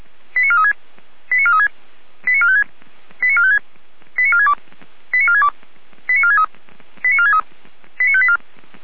Five-tone selective call (ZVEI), heard as quick runs of five tones, tones of about 70 ms. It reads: 89436 89436 89456 89456 89462 89462 89463 89463 89584
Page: ZVEI